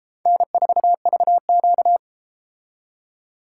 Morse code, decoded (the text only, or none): N4VQ